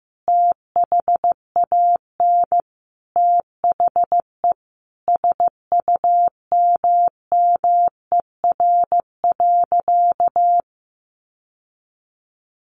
Morse code, heard as a keyed tone, about 15 wpm, 700 Hz. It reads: THAN THE SUMMER.
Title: Sentences from Top 1100 Words 15wpm